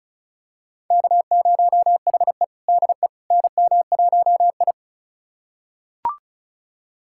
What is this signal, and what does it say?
Morse code, heard as a keyed tone, about 36 wpm, 700 Hz. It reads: K0HE DE NM1I
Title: USA Callsign DE Callsign - 4 Characters - 35wpm